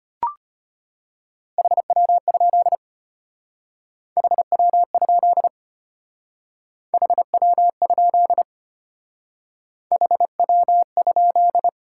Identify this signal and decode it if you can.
Morse code, heard as a keyed tone, about 32 wpm, 700 Hz. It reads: HW? HW? HW? HW?